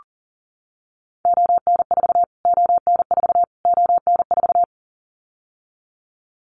Morse code, heard as a keyed tone, about 40 wpm, 700 Hz. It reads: ON4 ON4 ON4
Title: Callsign Prefixes for Top 25 DXCC - Repeated 3x - 40wpm